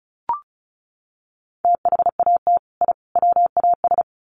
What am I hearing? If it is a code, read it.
Morse code, heard as a keyed tone, about 34 wpm, 700 Hz. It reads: THAT I WAS
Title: Sets of 3 Words 35wpm